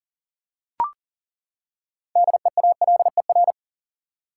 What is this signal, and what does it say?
Morse code, heard as a keyed tone, about 40 wpm, 700 Hz. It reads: DEALER
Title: Six Letter Words - 40wpm